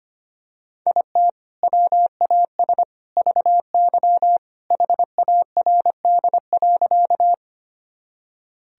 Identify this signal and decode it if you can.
Morse code, heard as a keyed tone, about 25 wpm, 700 Hz. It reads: IT WAS VY HARD.
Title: Sentences from Top 300 Words and Abbreviations 25wpm